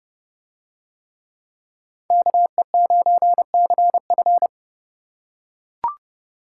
Morse code, heard as a keyed tone, about 29 wpm, 700 Hz. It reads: KE9CF